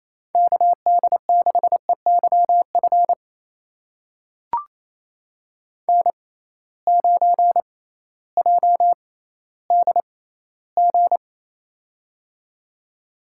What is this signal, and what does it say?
Morse code, heard as a keyed tone, about 28 wpm, spaced as for 15 wpm, 700 Hz. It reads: KD6EYF N9JDG
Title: USA Call Signs - 6x Character Spacing - 28wpm